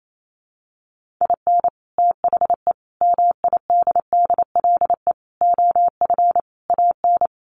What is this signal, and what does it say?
Morse code, heard as a keyed tone, about 28 wpm, 700 Hz. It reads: IN THE MIDDLE OF AN
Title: Sets of 5 Words Encore2 28wpm